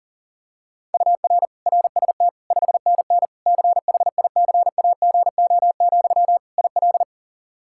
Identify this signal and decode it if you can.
Morse code, heard as a keyed tone, about 40 wpm, 700 Hz. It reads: UR RST 5NN CHICAGO, IL
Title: K1USN SST QSO Practice 40wpm